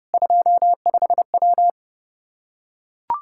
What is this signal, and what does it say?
Morse code, heard as a keyed tone, about 32 wpm, 700 Hz. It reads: 25W
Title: K1USN SST QSO Practice 30wpm